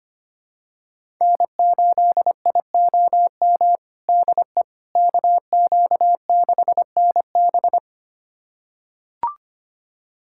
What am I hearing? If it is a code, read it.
Morse code, heard as a keyed tone, about 25 wpm, 700 Hz. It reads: N8IOM DE KQ6NB
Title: USA Callsign DE Callsign - 5 Characters - 25wpm